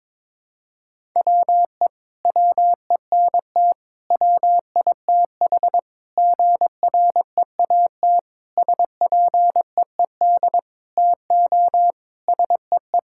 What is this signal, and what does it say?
Morse code, heard as a keyed tone, about 22 wpm, 700 Hz. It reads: WE WENT WITH GREAT SPEED TO SEE
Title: Sentences from Top 1000 Words - Review Part 5 - 22wpm